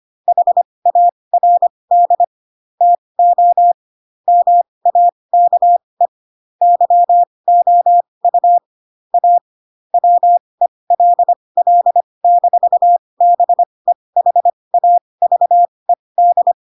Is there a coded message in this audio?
Yes: Morse code from 0.3 to 16.5 s, heard as a keyed tone, about 25 wpm, spaced as for 20 wpm, 700 Hz. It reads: HARD TO MAKE YOU A WELL-BEHAVED